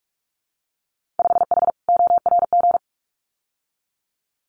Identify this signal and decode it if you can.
Morse code, heard as a keyed tone, about 46 wpm, 690 Hz. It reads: 5H ORG
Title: Winter Field Day Exchanges Encore 45wpm